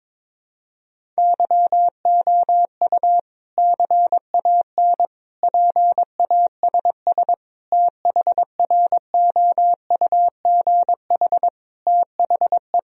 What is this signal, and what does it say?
Morse code, heard as a keyed tone, about 22 wpm, 700 Hz. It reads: YOU CAN PASS THROUGH THE
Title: Sentences from Top 1000 Words - Review Part 1 - 22wpm